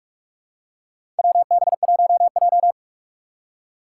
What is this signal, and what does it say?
Morse code, heard as a keyed tone, about 46 wpm, 700 Hz. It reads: WB1J
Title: USA Call Signs - 4 Characters - 4x Character Spacing - 45wpm